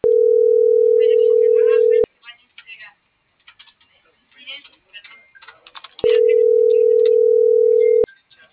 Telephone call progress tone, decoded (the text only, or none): ringback tone